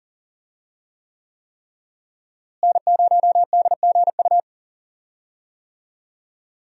Morse code, heard as a keyed tone, about 40 wpm, 700 Hz. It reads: N0DGU
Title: USA Call Signs - 5 Characters - Repeated 3x - 40wpm